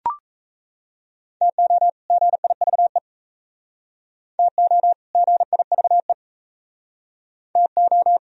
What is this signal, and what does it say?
Morse code, heard as a keyed tone, about 37 wpm, 700 Hz. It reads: TO GIVE TO GIVE TO